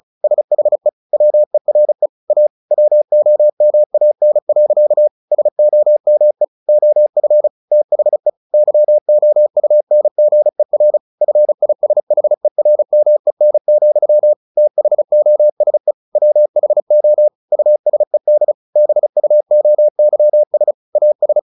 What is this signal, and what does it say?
Morse code, heard as a keyed tone, about 34 wpm, 600 Hz. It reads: SHE WERE A WOMAN. SOME OF THE YOUNGER FISHERMEN, THOSE WHO USED BUOYS AS
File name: the old man and the sea / the old man and the sea 1 35wpm